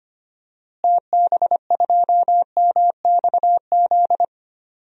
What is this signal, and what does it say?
Morse code, heard as a keyed tone, about 25 wpm, 700 Hz. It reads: TB2MXZ
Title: DXCC-100 Call Signs - 6 Characters - Repeated 2x - 25wpm